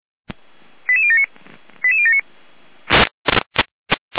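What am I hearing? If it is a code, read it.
Five-tone selective call (ZVEI), heard as quick runs of five tones, tones of about 70 ms. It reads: 88079 88079
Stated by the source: ZVEI five-tone call received on SDR.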